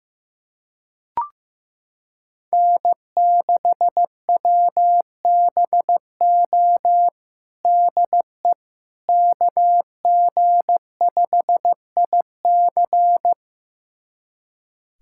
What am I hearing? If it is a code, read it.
Morse code, heard as a keyed tone, about 15 wpm, 700 Hz. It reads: N6WBO DE KG5IC
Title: USA Callsign DE Callsign - 5 Characters - 15wpm